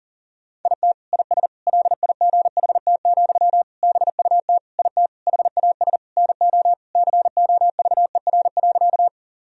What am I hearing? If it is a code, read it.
Morse code, heard as a keyed tone, about 40 wpm, 700 Hz. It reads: IT IS LIGHT, BUT IT HAS NO COVER.
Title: Sentences from Top 1000 Words - Review Part 1 - 40wpm